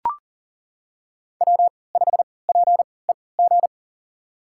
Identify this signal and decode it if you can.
Morse code, heard as a keyed tone, about 37 wpm, spaced as for 22 wpm, 700 Hz. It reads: W5PEG
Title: USA Call Signs - 5 Characters - 3x Character Spacing - 40wpm